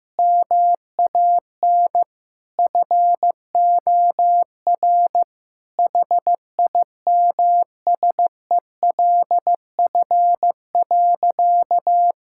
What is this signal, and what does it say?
Morse code, heard as a keyed tone, about 15 wpm, 700 Hz. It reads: MAN FOR HIMSELF.
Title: Sentences from Top 1600 Words 15wpm